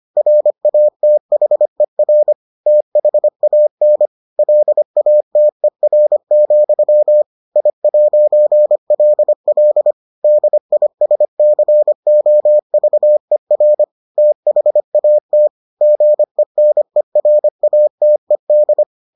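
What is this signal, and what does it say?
Morse code, heard as a keyed tone, about 25 wpm, 600 Hz. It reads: RATHER THAN LATER, I'LL DISCOVER THAT GENERATED
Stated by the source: Morse edition of the Foundations of Amateur Radio episodes.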